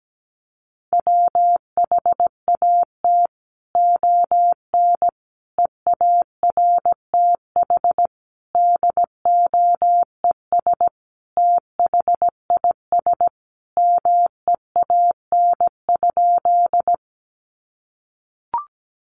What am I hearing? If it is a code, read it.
Morse code, heard as a keyed tone, about 17 wpm, 700 Hz. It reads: WHAT ON EARTH DOES THIS MEAN?
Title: Sentences from Top 200 Words 17wpm